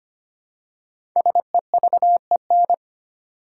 Morse code, heard as a keyed tone, about 25 wpm, 700 Hz. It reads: SEVEN
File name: 5 Letter Words - First 3 Letters Given - 25wpm